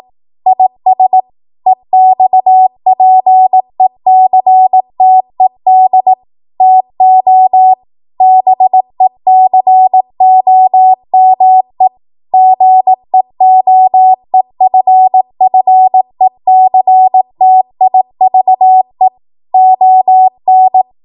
Morse code, heard as a keyed tone, about 18 wpm, 750 Hz. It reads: IS EXPECTED TO BECOME GEOEFFECTIVE ON
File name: Morse / 18WPM